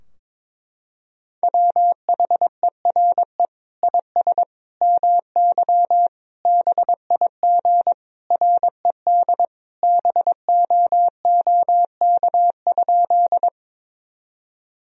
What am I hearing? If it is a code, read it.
Morse code, heard as a keyed tone, about 22 wpm, 700 Hz. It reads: WHERE IS MY BIG RED BOOK?
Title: Sentences from Top 1000 Words - Review Part 2 - 22wpm